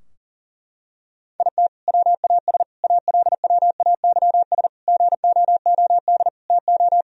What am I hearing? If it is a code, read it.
Morse code, heard as a keyed tone, about 40 wpm, 700 Hz. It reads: IT WAS ALWAYS GOOD TO
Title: Sentences from Top 1000 Words - Review Part 4 - 40wpm